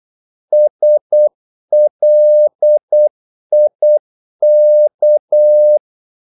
Morse code, heard as a keyed tone, about 8 wpm, 600 Hz. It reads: SLIK
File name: Korte ord DK 8wpm